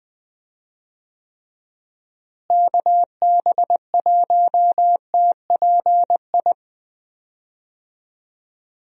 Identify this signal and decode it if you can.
Morse code, heard as a keyed tone, about 21 wpm, 700 Hz. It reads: KB1TPI
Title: USA Call Signs - 6 Characters - Repeated 3x - 20wpm